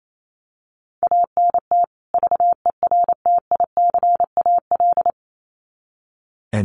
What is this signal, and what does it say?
Morse code, heard as a keyed tone, about 28 wpm, 700 Hz. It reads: ANT VERTICAL
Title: Common QSO Phrases 28wpm